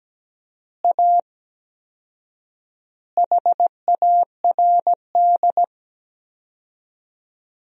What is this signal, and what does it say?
Morse code, heard as a keyed tone, about 17 wpm, 700 Hz. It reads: A HARD
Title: 4x Word-Spacing - Sentences from Top 300 Words - 17wpm